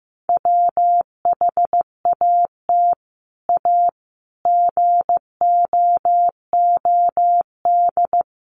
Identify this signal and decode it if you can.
Morse code, heard as a keyed tone, about 15 wpm, 700 Hz. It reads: WHAT A GOOD
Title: Sentences from Top 1100 Words 15wpm